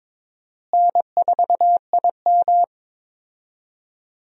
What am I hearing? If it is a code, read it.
Morse code, heard as a keyed tone, about 23 wpm, 700 Hz. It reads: N4IM
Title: USA Call Signs - 4 Characters - Repeated 2x - 22wpm